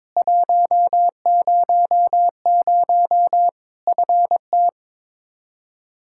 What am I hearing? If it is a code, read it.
Morse code, heard as a keyed tone, about 21 wpm, 700 Hz. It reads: 100 FT
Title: K1USN SST QSO Practice 22wpm